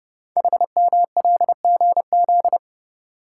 Morse code, heard as a keyed tone, about 30 wpm, 700 Hz. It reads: HMLGZ
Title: Code Groups - 5 Letters - 30wpm